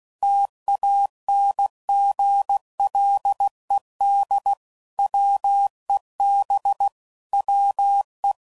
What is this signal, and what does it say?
Morse code, heard as a keyed tone, about 16 wpm, 800 Hz. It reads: TANGLED WEB WE
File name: Sayings / Quotes / Deceive